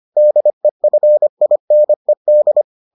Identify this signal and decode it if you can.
Morse code, heard as a keyed tone, about 25 wpm, 600 Hz. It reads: DEFINED